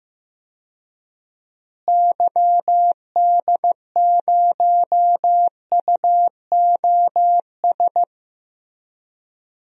Morse code, heard as a keyed tone, about 16 wpm, 700 Hz. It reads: YD0UOS
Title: DXCC-100 Call Signs - 6 Characters - Repeated 2x - 15wpm